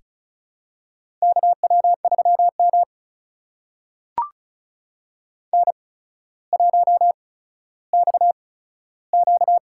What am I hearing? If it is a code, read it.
Morse code, heard as a keyed tone, about 36 wpm, spaced as for 15 wpm, 700 Hz. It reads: KW3M N1XQ
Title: USA Call Signs - 4 Characters - 8x Character Spacing - 35wpm